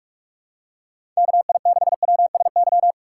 Morse code, heard as a keyed tone, about 46 wpm, 700 Hz. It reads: KI6WSY
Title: USA Call Signs - 6 Characters - 4x Character Spacing - 45wpm